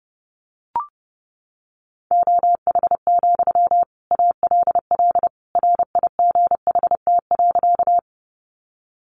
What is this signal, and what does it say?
Morse code, heard as a keyed tone, about 30 wpm, 700 Hz. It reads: OH, ALL RIGHT.
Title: Sentences from Top 500 Words 30wpm